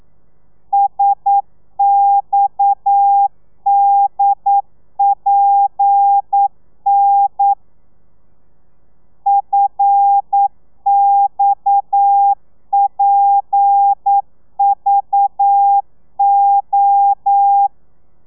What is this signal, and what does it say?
Morse code, heard as a keyed tone, about 9 wpm, 800 Hz. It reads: SXDPN FXPVO
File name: telegrafia / i2tfj / Lez7 40